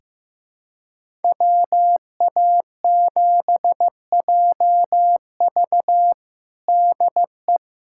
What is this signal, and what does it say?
Morse code, heard as a keyed tone, about 15 wpm, 700 Hz. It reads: WA7JV DE
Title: USA Callsign DE Callsign - 5 Characters - 15wpm